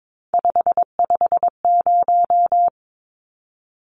Morse code, heard as a keyed tone, about 22 wpm, 700 Hz. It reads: 550
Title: Numbers - 3-Digit 22wpm